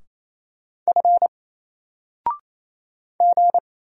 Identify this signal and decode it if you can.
Morse code, heard as a keyed tone, about 27 wpm, 700 Hz. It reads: F G
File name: Single-Letter-Number-28wpm